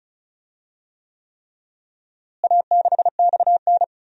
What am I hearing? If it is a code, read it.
Morse code, heard as a keyed tone, about 36 wpm, 700 Hz. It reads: A6XN
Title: DXCC-100 Call Signs - 4 Characters - Repeated 3x - 35wpm